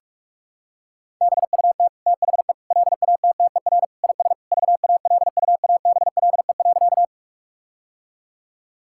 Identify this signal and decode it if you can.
Morse code, heard as a keyed tone, about 46 wpm, 700 Hz. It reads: BUT THE LATTER IS VALUABLE.
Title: Sentences from Top 1600 Words 45wpm